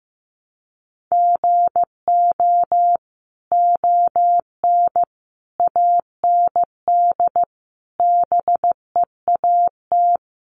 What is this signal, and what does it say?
Morse code, heard as a keyed tone, about 15 wpm, 700 Hz. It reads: GO ON AND BEAT
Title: Sentences from Top 1300 Words 15wpm